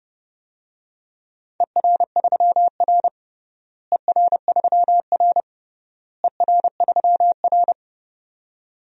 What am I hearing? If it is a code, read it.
Morse code, heard as a keyed tone, about 32 wpm, 700 Hz. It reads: ER3R ER3R ER3R
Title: DXCC-100 Call Signs - 4 Characters - Repeated 3x - 30wpm